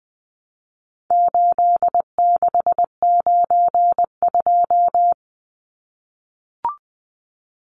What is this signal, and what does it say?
Morse code, heard as a keyed tone, about 20 wpm, 700 Hz. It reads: 8692
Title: Numbers - 4-Digit 20wpm